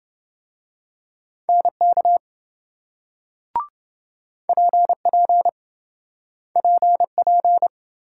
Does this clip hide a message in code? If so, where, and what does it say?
Yes: Morse code from 1.5 to 7.7 s, heard as a keyed tone, about 27 wpm, 700 Hz. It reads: NK PP PP